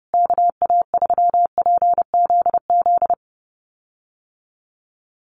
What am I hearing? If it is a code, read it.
Morse code, heard as a keyed tone, about 30 wpm, 700 Hz. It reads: KA3PZZ
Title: USA Call Signs Repeated 2x 30wpm